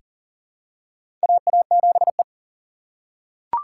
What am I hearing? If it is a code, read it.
Morse code, heard as a keyed tone, about 40 wpm, 700 Hz. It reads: AA7E
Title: USA Call Signs - 3x Character Spacing - 40wpm